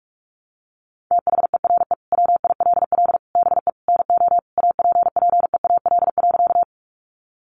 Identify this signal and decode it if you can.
Morse code, heard as a keyed tone, about 44 wpm, 700 Hz. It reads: THERE WILL BE NO APPEAL.
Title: Sentences from Top 1300 Words 45wpm